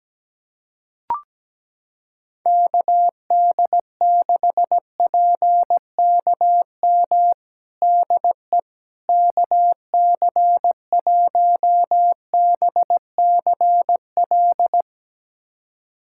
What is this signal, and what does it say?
Morse code, heard as a keyed tone, about 17 wpm, 700 Hz. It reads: KD6PKM DE KC1BCL
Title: USA Callsign DE Callsign 17wpm